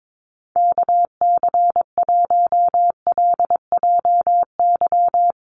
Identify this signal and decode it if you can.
Morse code, heard as a keyed tone, about 22 wpm, 700 Hz. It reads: KC1LJY
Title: USA Call Signs Repeated 3x 22wpm